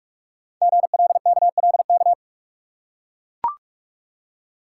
Morse code, heard as a keyed tone, about 44 wpm, 690 Hz. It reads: GLKLK
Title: Code Groups - 5 Letters - 45wpm